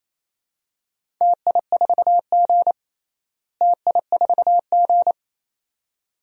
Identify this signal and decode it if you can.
Morse code, heard as a keyed tone, about 28 wpm, 700 Hz. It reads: TI4G TI4G
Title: DXCC-100 Call Signs - 4 Characters - Repeated 2x - 28wpm